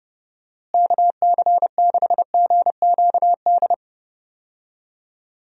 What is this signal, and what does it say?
Morse code, heard as a keyed tone, about 30 wpm, 700 Hz. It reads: KC6GQD
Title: USA Call Signs - 6 Characters - Repeated 2x - 30wpm